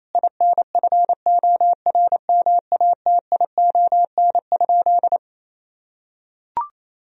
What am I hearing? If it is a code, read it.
Morse code, heard as a keyed tone, about 27 wpm, 700 Hz. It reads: INFORMATION?
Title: Sentences from Top 1000 Words - Review Part 4 - 28wpm